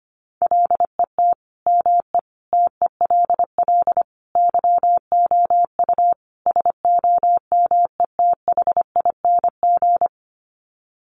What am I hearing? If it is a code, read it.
Morse code, heard as a keyed tone, about 25 wpm, 700 Hz. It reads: LET ME TELL YOU SOMETHING
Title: Sets of 5 Words 25wpm